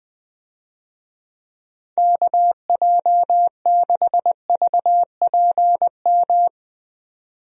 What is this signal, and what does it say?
Morse code, heard as a keyed tone, about 20 wpm, 700 Hz. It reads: KJ6VPM
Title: USA Call Signs - 6 Characters - 20wpm